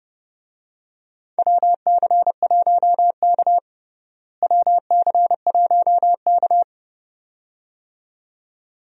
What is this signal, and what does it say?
Morse code, heard as a keyed tone, about 31 wpm, 700 Hz. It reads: WC1K WC1K
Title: USA Call Signs - 4 Characters - Repeated 2x - 30wpm